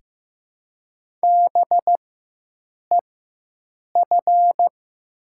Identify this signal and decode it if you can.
Morse code, heard as a keyed tone, about 15 wpm, spaced as for 7 wpm, 700 Hz. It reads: BEF
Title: 4x Character Spacing - Top 300 Words - 15wpm